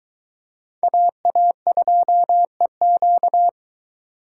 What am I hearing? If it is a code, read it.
Morse code, heard as a keyed tone, about 22 wpm, 700 Hz. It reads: AA2EQ